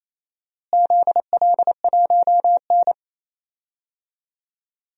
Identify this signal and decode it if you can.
Morse code, heard as a keyed tone, about 28 wpm, 700 Hz. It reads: ZL1N
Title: DXCC-100 Call Signs - 4 Characters - Repeated 3x - 28wpm